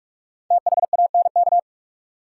Morse code, heard as a keyed tone, about 46 wpm, 700 Hz. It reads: THANK
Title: Sets of 2 Words - New Word Drill - 45wpm